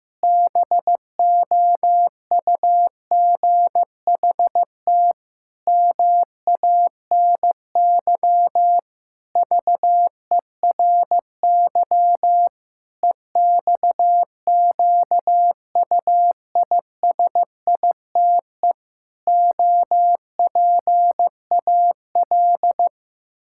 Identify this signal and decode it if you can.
Morse code, heard as a keyed tone, about 15 wpm, 700 Hz. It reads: BOUGHT MANY VERY EXQUISITE OPAL
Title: Pangram Challenge v2 - 15wpm